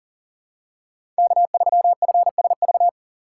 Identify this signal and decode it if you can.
Morse code, heard as a keyed tone, about 40 wpm, 700 Hz. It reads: K3FSV
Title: USA Call Signs - 5 Characters - 6x Character Spacing - 40wpm